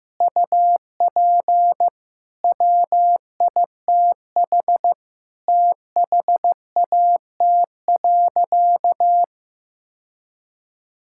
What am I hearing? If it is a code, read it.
Morse code, heard as a keyed tone, about 16 wpm, 700 Hz. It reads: UP WITH THAT.
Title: Sentences from Top 1000 Words - Review Part 5 - 15wpm